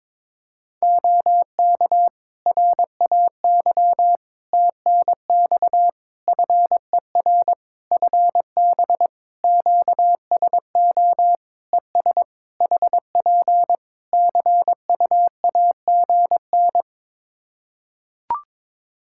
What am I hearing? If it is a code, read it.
Morse code, heard as a keyed tone, about 22 wpm, 700 Hz. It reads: OK RAY TNX FER FB QSO ES HP CUAGN
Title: Practice QSO - Exchange 4 - Ending - 22wpm